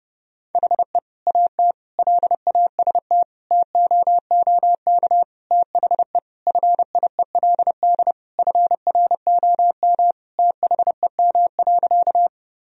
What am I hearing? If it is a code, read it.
Morse code, heard as a keyed tone, about 30 wpm, 700 Hz. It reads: HE AT LAST TOOK THE FIELD FROM THEM.
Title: Sentences from Top 1000 Words - Review Part 2 - 30wpm